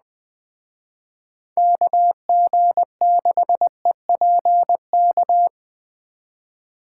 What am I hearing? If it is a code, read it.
Morse code, heard as a keyed tone, about 20 wpm, 700 Hz. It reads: KG6EPK
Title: Speed Racing - USA 2x3 Call Signs 17wpm